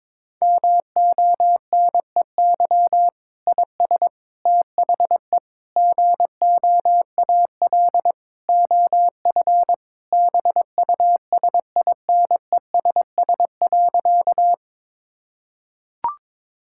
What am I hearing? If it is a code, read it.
Morse code, heard as a keyed tone, about 21 wpm, 700 Hz. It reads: MONEY IS THE GOAL OF BUSINESS.
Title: Sentences from Top 1000 Words - Review Part 5 - 22wpm